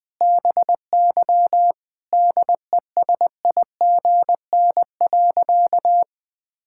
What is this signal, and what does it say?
Morse code, heard as a keyed tone, about 21 wpm, 700 Hz. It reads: BY DESIGN.
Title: Sentences from Top 1000 Words - Review Part 5 - 20wpm